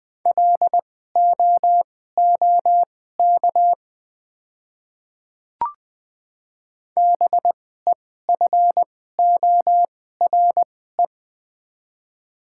Morse code, heard as a keyed tone, about 20 wpm, spaced as for 15 wpm, 690 Hz. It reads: LOOK BEFORE
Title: 2x Character Spacing - Top 300 Words - 20wpm